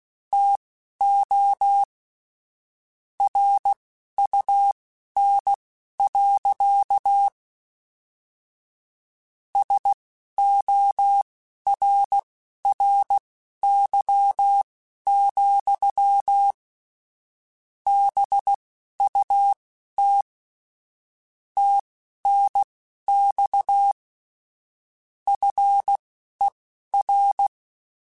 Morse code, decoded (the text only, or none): TO RUN. SORRY, BUT TNX FER